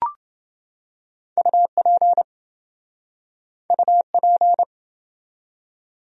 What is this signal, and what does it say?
Morse code, heard as a keyed tone, about 27 wpm, 700 Hz. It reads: UP UP